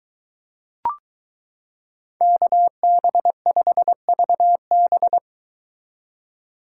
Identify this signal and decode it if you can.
Morse code, heard as a keyed tone, about 23 wpm, 700 Hz. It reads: KB5VB